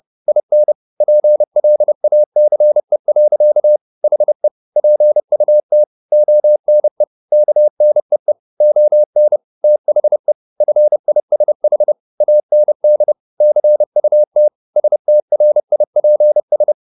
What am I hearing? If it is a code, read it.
Morse code, heard as a keyed tone, about 30 wpm, 600 Hz. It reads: IN PLACE. HE PUT ONE KNEE ON THE FISH AND CUT STRIPS
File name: the old man and the sea / the old man and the sea 1 30wpm